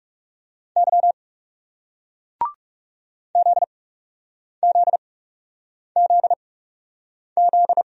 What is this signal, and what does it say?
Morse code, heard as a keyed tone, about 40 wpm, spaced as for 8 wpm, 700 Hz. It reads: Y ZZZZ